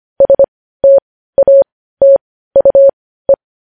Morse code, heard as a keyed tone, about 25 wpm, spaced as for 15 wpm, 550 Hz. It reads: STATUE
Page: News Headlines in Morse Code at 15 WPM